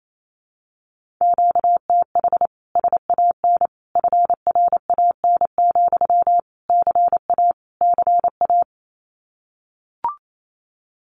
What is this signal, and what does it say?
Morse code, heard as a keyed tone, about 28 wpm, 700 Hz. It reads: QTH SAN FRAN, CA CA
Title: Common QSO Phrases Encore 28wpm